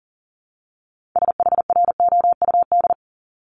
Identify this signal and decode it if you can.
Morse code, heard as a keyed tone, about 39 wpm, 700 Hz. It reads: SHROUD
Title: Multi-letter Phonemes 40wpm